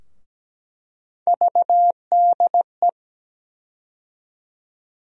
Morse code, heard as a keyed tone, about 17 wpm, 700 Hz. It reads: VDE
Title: USA Call Sign Postfixes - 3 Letters 17wpm